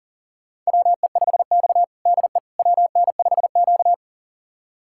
Morse code, heard as a keyed tone, about 40 wpm, 700 Hz. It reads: WE5X DE WN5Q
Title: USA Callsign DE Callsign - 4 Characters - 40wpm